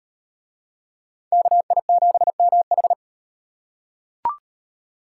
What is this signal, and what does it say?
Morse code, heard as a keyed tone, about 36 wpm, 700 Hz. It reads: KI7MH